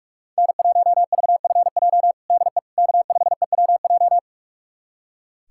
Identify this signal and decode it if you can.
Morse code, heard as a keyed tone, about 46 wpm, 700 Hz. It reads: N1VFJ DE K5EWJ
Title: USA Callsign DE Callsign - 5 Characters - 45wpm